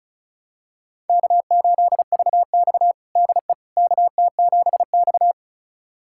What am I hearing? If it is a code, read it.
Morse code, heard as a keyed tone, about 34 wpm, 700 Hz. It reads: K8VX DE KT7X
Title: USA Callsign DE Callsign - 4 Characters - 35wpm